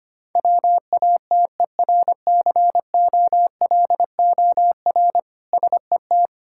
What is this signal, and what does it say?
Morse code, heard as a keyed tone, about 25 wpm, 700 Hz. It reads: WATERCOLOR SET
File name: practice / 25wpm / Difficult Words with Hints - 25wpm